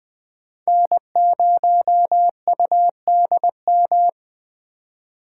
Morse code, heard as a keyed tone, about 20 wpm, 700 Hz. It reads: N0UDM
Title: Speed Racing - USA 1x3 Call Signs 17wpm